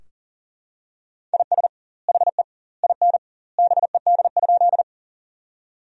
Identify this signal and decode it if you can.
Morse code, heard as a keyed tone, about 40 wpm, 700 Hz. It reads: IS HE IN BED?
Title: 2x Word-Spacing - Sentences from Top 500 Words - 40wpm